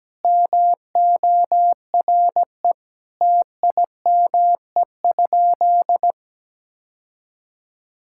Morse code, heard as a keyed tone, about 17 wpm, 700 Hz. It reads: MORE TIME?
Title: Sentences from Top 1000 Words - Review Part 5 - 17wpm